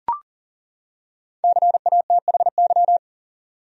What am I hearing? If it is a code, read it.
Morse code, heard as a keyed tone, about 39 wpm, 700 Hz. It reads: CATHY
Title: USA Common Names - Max 5 Letters - 40wpm